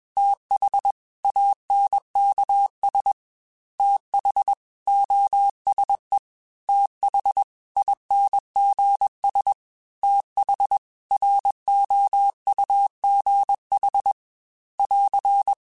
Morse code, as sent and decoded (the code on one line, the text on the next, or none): - .... / .- -. -.- ... / - .... / --- ... / - .... / .. -. --. ... / - .... / .-. --- ..- --. .... / .-.-.
TH ANKS TH OSE TH INGS TH ROUGH <AR>